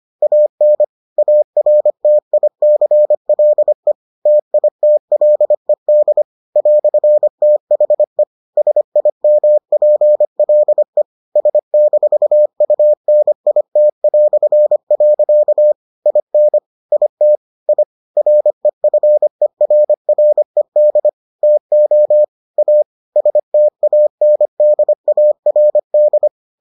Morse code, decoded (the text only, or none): AN ARTICLE TITLED "THE SIMPLE S-UNIT". IN IT I REFERRED TO A STANDARD